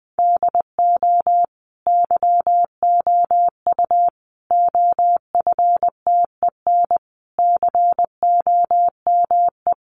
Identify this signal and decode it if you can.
Morse code, heard as a keyed tone, about 20 wpm, 700 Hz. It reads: DO YOU OFTEN COME
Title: Sentences from Top 300 Words 20wpm